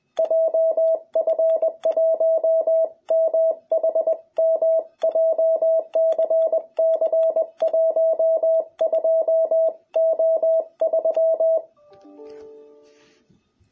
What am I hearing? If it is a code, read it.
Morse code, heard as a keyed tone, about 20 wpm, 660 Hz. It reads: JF1MHM JCC12O3
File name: jh8cbh / 2021cw